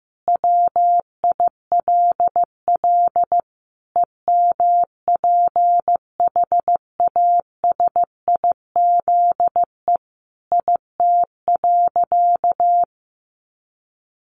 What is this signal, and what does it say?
Morse code, heard as a keyed tone, about 15 wpm, 700 Hz. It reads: WILL EMPHASIZE IT.
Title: Sentences from Top 1500 Words 15wpm